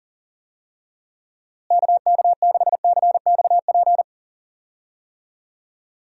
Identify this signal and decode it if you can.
Morse code, heard as a keyed tone, about 37 wpm, 700 Hz. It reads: KK6CXP
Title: USA Call Signs - 6 Characters - Repeated 2x - 40wpm